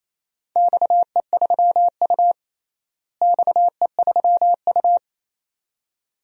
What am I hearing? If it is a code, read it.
Morse code, heard as a keyed tone, about 28 wpm, 700 Hz. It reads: XE3U XE3U
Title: DXCC-100 Call Signs - 4 Characters - Repeated 2x - 28wpm